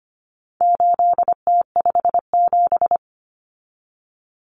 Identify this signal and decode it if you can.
Morse code, heard as a keyed tone, about 25 wpm, 700 Hz. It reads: 8T57
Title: Common Cut Numbers - 4-Digit 25wpm